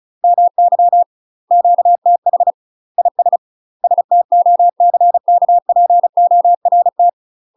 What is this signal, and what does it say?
Morse code, heard as a keyed tone, about 34 wpm, 700 Hz. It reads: MY QTH IS STOCKPORT
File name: UK MASTER QSOs TRAINING/35WPM